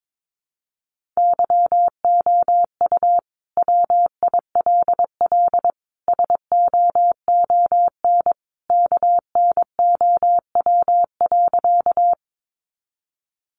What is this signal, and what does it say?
Morse code, heard as a keyed tone, about 22 wpm, 700 Hz. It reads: YOU WILL SOON KNOW.
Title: Sentences from Top 400 Words 22wpm